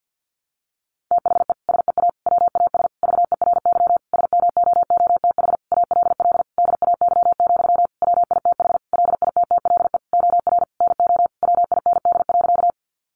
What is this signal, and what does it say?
Morse code, heard as a keyed tone, about 50 wpm, 700 Hz. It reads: THE SEA WAS VERY SMOOTH ALL DAY, WITH LITTLE OR NO WIND.
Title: Sentences from Top 1400 Words 50wpm